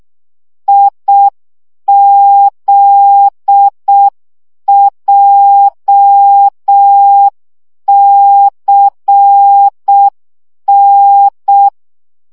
Morse code, heard as a keyed tone, about 6 wpm, 800 Hz. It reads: IZJCN
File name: morze vizsga 2011-2 5